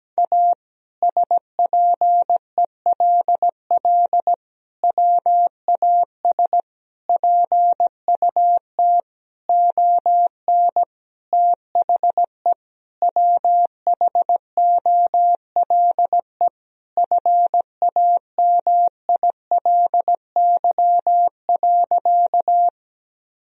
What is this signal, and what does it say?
Morse code, heard as a keyed tone, about 17 wpm, 700 Hz. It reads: A SPELL WAS PUT ON THE WHOLE FAMILY.
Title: Sentences from Top 1000 Words - Review Part 1 - 17wpm